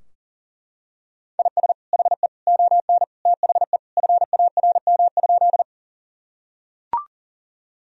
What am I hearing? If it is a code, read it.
Morse code, heard as a keyed tone, about 40 wpm, 700 Hz. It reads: IS HE ON THE FARM?
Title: Sentences from Top 1000 Words - Review Part 2 - 40wpm